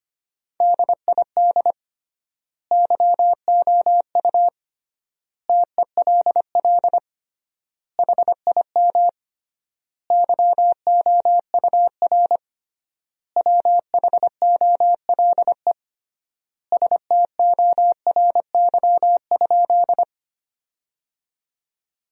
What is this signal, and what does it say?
Morse code, heard as a keyed tone, about 25 wpm, 700 Hz. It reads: DID YOU TELL HIM YOUR WHOLE STORY?
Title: 3x Word-Spacing - Sentences from Top 400 Words - 25wpm